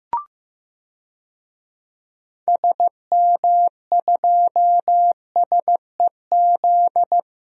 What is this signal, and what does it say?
Morse code, heard as a keyed tone, about 15 wpm, 700 Hz. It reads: SM2SEZ
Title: DXCC-100 Call Signs - 6 Characters - 15wpm